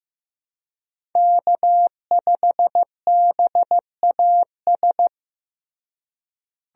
Morse code, heard as a keyed tone, about 15 wpm, 700 Hz. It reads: K5BAS
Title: USA Call Signs - 5 Characters - Repeated 3x - 15wpm